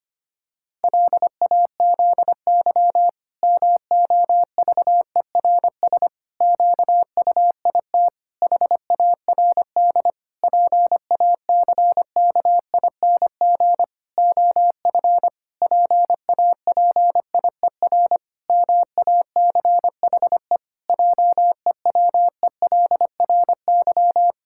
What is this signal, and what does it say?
Morse code, heard as a keyed tone, about 25 wpm, 700 Hz. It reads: LAZY MOVERS QUIT HARD PACKING OF PAPIER MACHE JEWELRY
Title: Pangram Challenge v2 - 25wpm